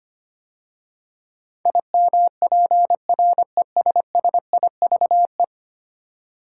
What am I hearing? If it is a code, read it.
Morse code, heard as a keyed tone, about 25 wpm, 700 Hz. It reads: IMPRESSIVE
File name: 10 Letter Words - First 3 Letters Given - 25wpm